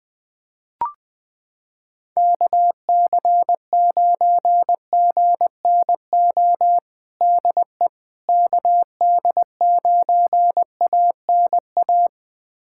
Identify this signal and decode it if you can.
Morse code, heard as a keyed tone, about 20 wpm, 700 Hz. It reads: KC9GNO DE KD9ANA
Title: USA Callsign DE Callsign - 6 Characters - 20wpm